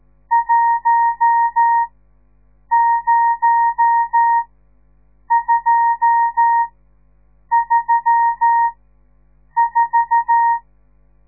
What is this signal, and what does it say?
Morse code, heard as a keyed tone, about 14 wpm, spaced as for 7 wpm, 930 Hz. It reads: 10234
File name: radiomorningstar / morse / les 10